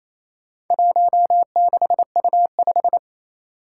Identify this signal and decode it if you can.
Morse code, heard as a keyed tone, about 27 wpm, 700 Hz. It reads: 16U5
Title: Four Challenging Letter-Number Combinations 28wpm